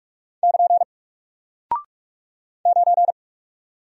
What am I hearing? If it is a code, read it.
Morse code, heard as a keyed tone, about 44 wpm, 700 Hz. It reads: <KN> 9
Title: Single Character (Letter:Number:Punctuation:Prosign) Alt 45wpm